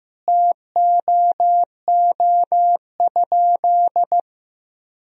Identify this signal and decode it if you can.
Morse code, heard as a keyed tone, about 15 wpm, 700 Hz. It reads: TOO?
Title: Sentences from Top 200 Words and Abbreviations 15wpm